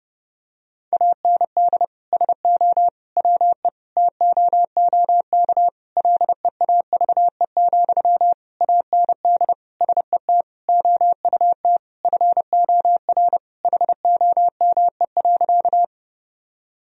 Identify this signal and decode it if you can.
Morse code, heard as a keyed tone, about 32 wpm, 700 Hz. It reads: AND SO WE TOOK LEAVE, AND SET OUT FOR HOME.
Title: Sentences from Top 1000 Words - Review Part 1 - 30wpm